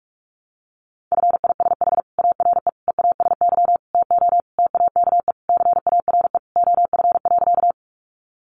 Morse code, heard as a keyed tone, about 44 wpm, 700 Hz. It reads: FISH ARE EASY TO TAKE CARE OF.
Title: Additional Easy Phrases 45wpm